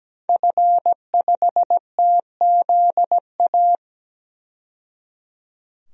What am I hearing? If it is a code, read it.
Morse code, heard as a keyed tone, about 17 wpm, 700 Hz. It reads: F5TZA
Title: DXCC-100 Call Signs - 5 Characters - 17wpm